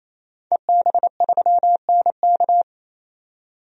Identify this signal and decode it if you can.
Morse code, heard as a keyed tone, about 28 wpm, 700 Hz. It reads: EB3NK
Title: DXCC-100 Call Signs - 5 Characters - 28wpm